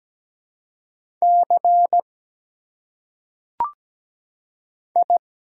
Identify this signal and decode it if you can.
Morse code, heard as a keyed tone, about 17 wpm, 700 Hz. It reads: C I
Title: Single Letters - Phonetic Pronunciation 17wpm